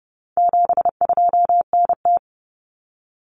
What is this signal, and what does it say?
Morse code, heard as a keyed tone, about 30 wpm, 700 Hz. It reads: 72NT
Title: Common Cut Numbers - 4-Digit 30wpm